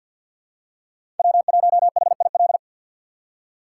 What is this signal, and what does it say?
Morse code, heard as a keyed tone, about 47 wpm, 700 Hz. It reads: W1HIL
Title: USA Call Signs - 5 Characters - 4x Character Spacing - 50wpm